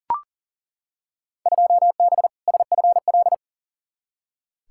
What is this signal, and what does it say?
Morse code, heard as a keyed tone, about 40 wpm, 700 Hz. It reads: 2B SFL
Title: Field Day Exchanges Encore - 40wpm